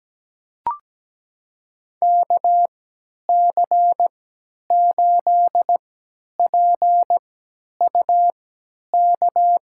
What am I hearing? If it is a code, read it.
Morse code, heard as a keyed tone, about 17 wpm, spaced as for 10 wpm, 700 Hz. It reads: KC8PUK
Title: USA Call Signs - 6 Characters - 3x Character Spacing - 17wpm